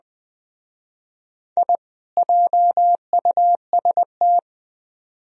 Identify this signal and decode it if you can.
Morse code, heard as a keyed tone, about 20 wpm, 700 Hz. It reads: I JUST
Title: Speed-Racing - Sets of 2 Words 17wpm